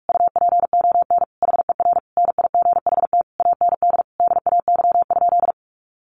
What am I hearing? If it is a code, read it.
Morse code, heard as a keyed tone, about 44 wpm, 700 Hz. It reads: UPON HER NIGHT AND DAY?
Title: Sentences from Top 1100 Words 45wpm